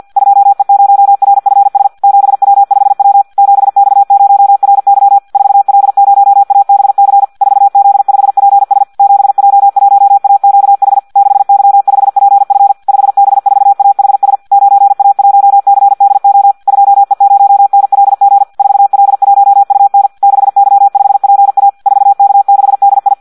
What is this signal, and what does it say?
Morse code, heard as a keyed tone, about 50 wpm, 800 Hz. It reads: 2E0RFS 7W5W 7X0RY 4L0ABC 4Z5PS 7J1AQH 6Y5PF 5B4AHS 9A1CDO 2E0NLG 4L1UN 6Y5PA 4K6DI
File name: Audio practica CW / qrq-tx-50 2